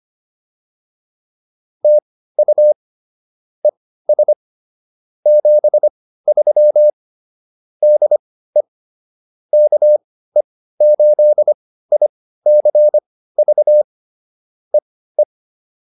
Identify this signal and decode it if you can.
Morse code, heard as a keyed tone, about 25 wpm, spaced as for 15 wpm, 600 Hz. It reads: TU ES 73 DE KE8ICV EE